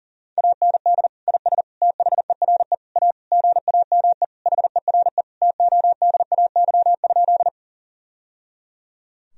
Morse code, heard as a keyed tone, about 40 wpm, 700 Hz. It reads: AND IS THERE A GAME HERE TODAY?
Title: Sentences from Top 1000 Words - Review Part 3 - 40wpm